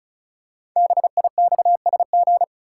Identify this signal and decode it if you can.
Morse code, heard as a keyed tone, about 36 wpm, 700 Hz. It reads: BIXSG
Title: Code Groups - 5 Letters - 35wpm